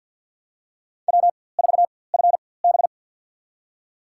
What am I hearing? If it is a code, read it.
Morse code, heard as a keyed tone, about 51 wpm, spaced as for 23 wpm, 700 Hz. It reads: W4FB
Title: USA Call Signs - 4x Character Spacing - 50wpm